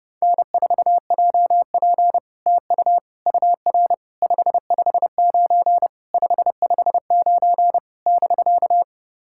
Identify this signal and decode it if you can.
Morse code, heard as a keyed tone, about 30 wpm, 700 Hz. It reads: N4JP TU UR 559 559 <BK>
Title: SOTA QSO Practice 30wpm